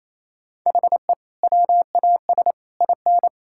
Morse code, heard as a keyed tone, about 28 wpm, 700 Hz. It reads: HE WAS IN
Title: Sentences from Top 1000 Words - Review Part 5 - 28wpm